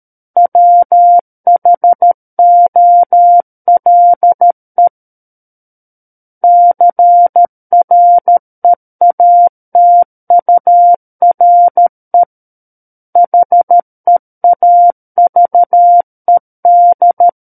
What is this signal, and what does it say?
Morse code, heard as a keyed tone, about 13 wpm, 700 Hz. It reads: WHOLE CREATURE HEAVED
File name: Book One / 04-Chapter 4 - 13 WPM